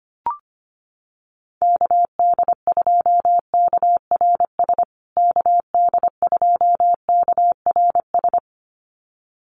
Morse code, heard as a keyed tone, about 25 wpm, 700 Hz. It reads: KD2KRS KD2KRS
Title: USA Call Signs Repeated 2x 25wpm